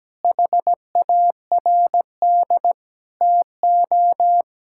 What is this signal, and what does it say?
Morse code, heard as a keyed tone, about 17 wpm, 700 Hz. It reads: HARD TO
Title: Sentences from Top 1000 Words - Review Part 5 - 17wpm